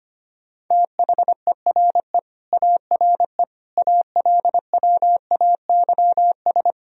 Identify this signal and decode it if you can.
Morse code, heard as a keyed tone, about 25 wpm, 700 Hz. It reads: THERE ARE ALWAYS
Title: Sentences from Top 1000 Words - Review Part 4 - 25wpm